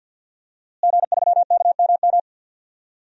Morse code, heard as a keyed tone, about 49 wpm, 700 Hz. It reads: G3KMM
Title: DXCC-100 Call Signs - 5 Characters - Repeated 3x - 50wpm